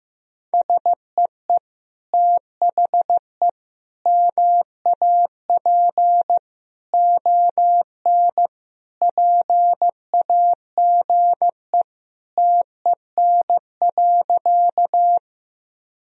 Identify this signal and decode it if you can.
Morse code, heard as a keyed tone, about 15 wpm, 700 Hz. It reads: SEE THE MAP ON PAGE TEN.
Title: Sentences from Top 1000 Words - Review Part 2 - 15wpm